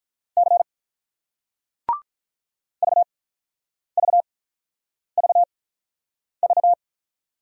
Morse code, heard as a keyed tone, about 48 wpm, spaced as for 9 wpm, 690 Hz. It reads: C VVVV